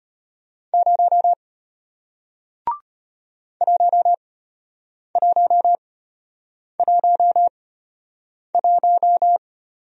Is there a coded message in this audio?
Yes: Morse code from 0.7 to 9.4 s, heard as a keyed tone, about 33 wpm, spaced as for 8 wpm, 700 Hz. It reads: 0 1111